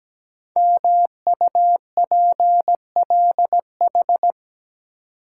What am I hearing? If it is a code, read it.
Morse code, heard as a keyed tone, about 17 wpm, 700 Hz. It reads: MUPLH
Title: Code Groups - 5 Letters - 17wpm